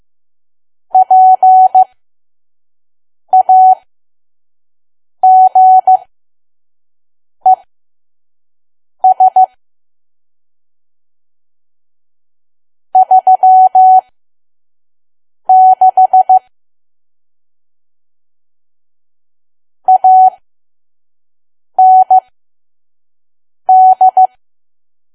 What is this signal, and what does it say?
Morse code, heard as a keyed tone, about 15 wpm, spaced as for 5 wpm, 750 Hz. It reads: PAGES 36 AND
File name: Archive / 5 WPM files / PM